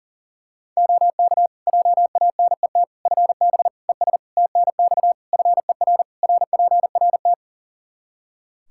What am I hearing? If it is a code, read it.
Morse code, heard as a keyed tone, about 40 wpm, 700 Hz. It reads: OK JANET FB ES TNX FER RPRT
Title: Practice QSO - Exchange 2 - Info - 40wpm